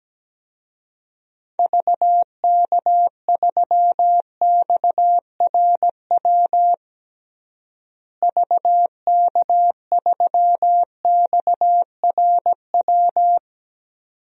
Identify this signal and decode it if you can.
Morse code, heard as a keyed tone, about 17 wpm, 700 Hz. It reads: VK3XRW VK3XRW
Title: DXCC-100 Call Signs - 6 Characters - Repeated 2x - 17wpm